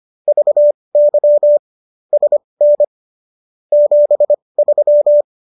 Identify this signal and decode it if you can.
Morse code, heard as a keyed tone, about 25 wpm, spaced as for 20 wpm, 590 Hz. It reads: VY SN 73